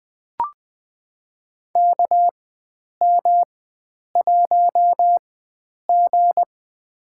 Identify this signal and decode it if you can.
Morse code, heard as a keyed tone, about 20 wpm, spaced as for 9 wpm, 700 Hz. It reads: KM1G
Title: USA Call Signs - 4 Characters - 4x Character Spacing - 20wpm